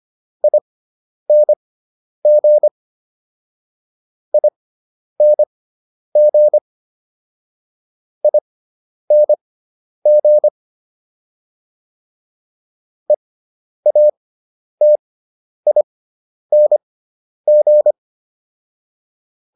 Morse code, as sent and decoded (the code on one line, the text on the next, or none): .. -. --. / .. -. --. / .. -. --. / . .- - .. -. --.
ING ING ING EATING